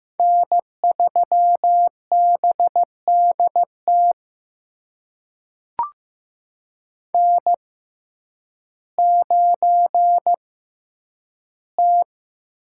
Morse code, heard as a keyed tone, about 15 wpm, spaced as for 9 wpm, 700 Hz. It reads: N3BDT N9T
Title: USA Call Signs - 5 Characters - 6x Character Spacing - 15wpm